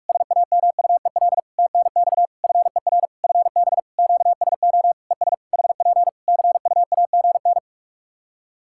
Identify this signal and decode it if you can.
Morse code, heard as a keyed tone, about 41 wpm, 700 Hz. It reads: SAMUEL TNX FER FB QSO ES HP CUAGN
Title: K1USN SST QSO Practice 45wpm